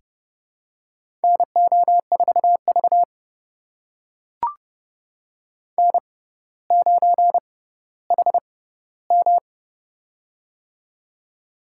Morse code, heard as a keyed tone, about 31 wpm, spaced as for 15 wpm, 700 Hz. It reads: NO4V N9HM
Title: USA Call Signs - 4 Characters - 6x Character Spacing - 30wpm